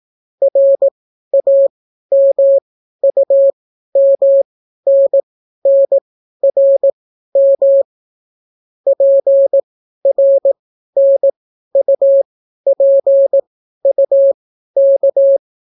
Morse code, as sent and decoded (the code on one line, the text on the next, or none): .-. .- -- ..- -- -. -. .-. -- / .--. .-. -. ..- .--. ..- -.-
RAMUMNNRM PRNUPUK